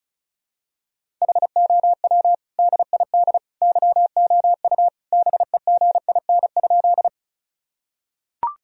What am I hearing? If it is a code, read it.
Morse code, heard as a keyed tone, about 34 wpm, 700 Hz. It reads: HOW DID YOU BEGIN?
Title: Sentences from Top 1000 Words - Review Part 1 - 35wpm